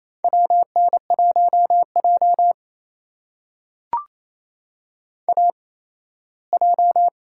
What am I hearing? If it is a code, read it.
Morse code, heard as a keyed tone, about 28 wpm, spaced as for 17 wpm, 700 Hz. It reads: WN1J AJ
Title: USA Call Signs - 4 Characters - 8x Character Spacing - 28wpm